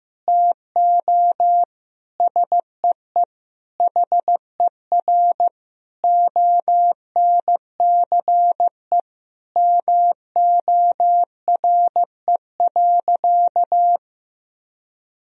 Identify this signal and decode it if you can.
Morse code, heard as a keyed tone, about 15 wpm, 700 Hz. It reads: TO SEE HER ONCE MORE.
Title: Sentences from Top 500 Words and Abbreviations 15wpm